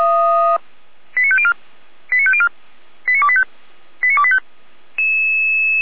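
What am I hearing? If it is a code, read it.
Five-tone selective call (ZVEI), heard as quick runs of five tones, tones of about 70 ms. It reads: 89504 89504 89285 89285
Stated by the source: Received on SDR, ZVEI five-tone selective call 89121.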